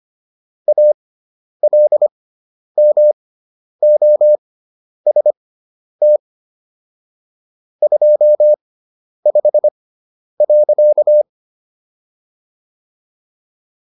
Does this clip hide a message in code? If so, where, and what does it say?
Yes: Morse code from 0.7 to 11.2 s, heard as a keyed tone, about 25 wpm, spaced as for 10 wpm, 600 Hz. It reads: ALMOST 25.